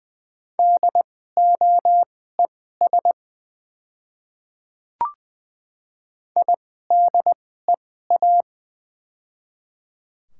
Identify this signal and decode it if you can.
Morse code, heard as a keyed tone, about 20 wpm, spaced as for 15 wpm, 690 Hz. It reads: DOES IDEA
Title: 2x Character Spacing - Top 500 Words - 20wpm